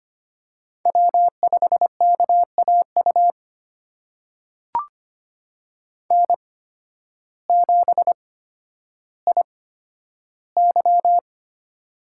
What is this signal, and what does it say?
Morse code, heard as a keyed tone, about 25 wpm, spaced as for 12 wpm, 700 Hz. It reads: W5KAU N7IY
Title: USA Call Signs - 8x Character Spacing - 25wpm